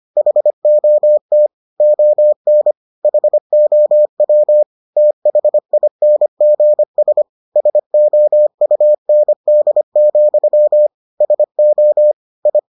Morse code, heard as a keyed tone, about 25 wpm, 600 Hz. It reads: HOT ON HOW THINGS SOUND, SO I